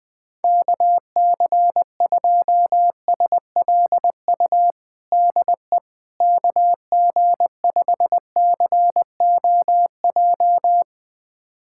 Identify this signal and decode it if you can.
Morse code, heard as a keyed tone, about 20 wpm, 700 Hz. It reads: KC2SLU DE KG5COJ
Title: USA Callsign DE Callsign - 6 Characters - 20wpm